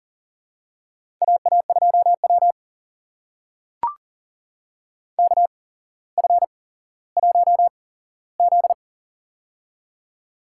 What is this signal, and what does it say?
Morse code, heard as a keyed tone, about 40 wpm, spaced as for 17 wpm, 700 Hz. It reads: AA2W KF1Z
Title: USA Call Signs - 4 Characters - 8x Character Spacing - 40wpm